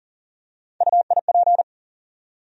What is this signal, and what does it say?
Morse code, heard as a keyed tone, about 40 wpm, 700 Hz. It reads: UIP
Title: USA Call Sign Postfixes - 3 Letters 40wpm